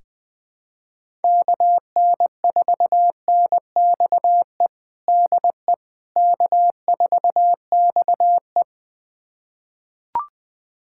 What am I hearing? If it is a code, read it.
Morse code, heard as a keyed tone, about 20 wpm, 700 Hz. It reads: KN4NXE DE K4XE
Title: USA Callsign DE Callsign 20wpm